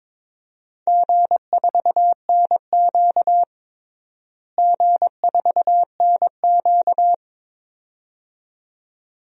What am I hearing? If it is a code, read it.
Morse code, heard as a keyed tone, about 22 wpm, 700 Hz. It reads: G4NQ G4NQ
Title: DXCC-100 Call Signs - 4 Characters - Repeated 2x - 22wpm